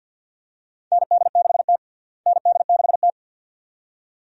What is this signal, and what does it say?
Morse code, heard as a keyed tone, about 50 wpm, 700 Hz. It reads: ND6T ND6T
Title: USA Call Signs - 4 Characters - Repeated 2x - 50wpm